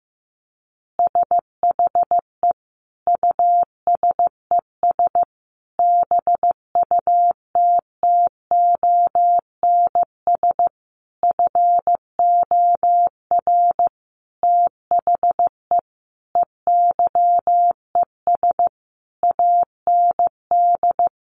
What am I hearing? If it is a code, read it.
Morse code, heard as a keyed tone, about 14 wpm, 700 Hz. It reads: SHE USES BUTTONS FOR THE EYES AND
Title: Easy Phrases 15wpm